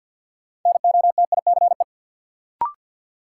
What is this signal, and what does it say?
Morse code, heard as a keyed tone, about 53 wpm, 700 Hz. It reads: NOTICE
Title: Sentences from Top 400 Words - New Word Drill - 50wpm